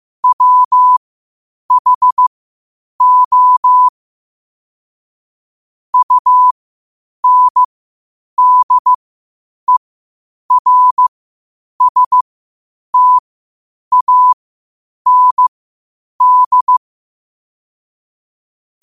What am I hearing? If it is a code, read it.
Morse code, heard as a keyed tone, about 15 wpm, spaced as for 8 wpm, 1000 Hz. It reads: WHO UNDERSTAND
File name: qotdfiles / 8wpm